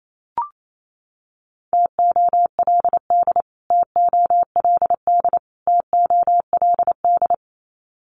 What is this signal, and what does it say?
Morse code, heard as a keyed tone, about 28 wpm, 700 Hz. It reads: TOLD TOLD TOLD
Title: Top 500 Words Repeated 28wpm